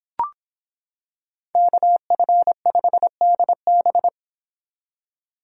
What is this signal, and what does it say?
Morse code, heard as a keyed tone, about 25 wpm, 700 Hz. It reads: KF5DB